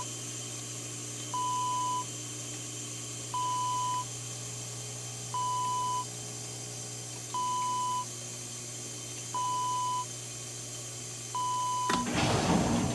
There is no code.